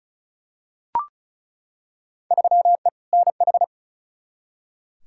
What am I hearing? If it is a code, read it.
Morse code, heard as a keyed tone, about 36 wpm, 700 Hz. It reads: 3E NH
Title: Field Day Exchanges Encore - 35wpm